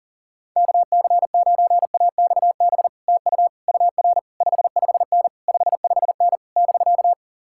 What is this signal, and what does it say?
Morse code, heard as a keyed tone, about 51 wpm, 700 Hz. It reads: KC9AXB TU UR 55N 55N <BK>